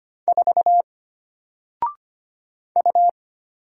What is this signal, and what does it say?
Morse code, heard as a keyed tone, about 25 wpm, 700 Hz. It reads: 4 U
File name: Single-Letter-Number-25wpm